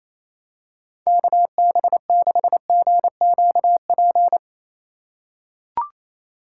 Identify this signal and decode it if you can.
Morse code, heard as a keyed tone, about 28 wpm, 700 Hz. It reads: KB6GQP
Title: USA Call Signs - 6 Characters - 4x Character Spacing - 28wpm